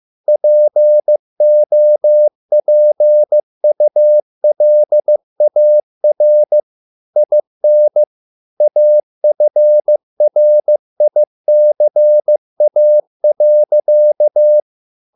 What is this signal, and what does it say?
Morse code, heard as a keyed tone, about 15 wpm, 600 Hz. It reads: POPULAR IN AFRICA.